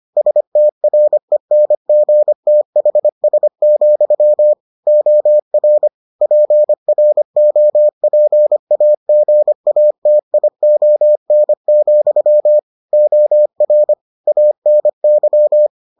Morse code, heard as a keyed tone, about 25 wpm, 600 Hz. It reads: STRENGTHS, OR PROPAGATION, OR ANY